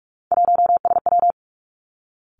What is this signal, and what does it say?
Morse code, heard as a keyed tone, about 46 wpm, 700 Hz. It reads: 1SW
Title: Three Challenging Letter-Number Combinations 45wpm